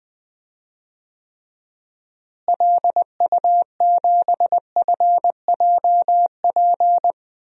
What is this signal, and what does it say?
Morse code, heard as a keyed tone, about 20 wpm, 700 Hz. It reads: LU7FJP
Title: DXCC-100 Call Signs - 6 Characters - Repeated 3x - 20wpm